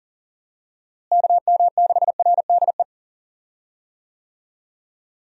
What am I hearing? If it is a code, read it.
Morse code, heard as a keyed tone, about 37 wpm, 700 Hz. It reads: KM6RDE
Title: USA Call Signs - 6 Characters - Repeated 2x - 40wpm